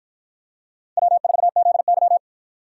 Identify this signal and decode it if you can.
Morse code, heard as a keyed tone, about 51 wpm, 700 Hz. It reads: W4ZY